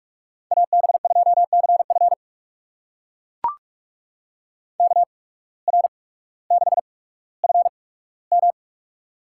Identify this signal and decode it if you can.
Morse code, heard as a keyed tone, about 44 wpm, spaced as for 19 wpm, 700 Hz. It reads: AB2CF KR6FM
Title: USA Call Signs - 5 Characters - 8x Character Spacing - 45wpm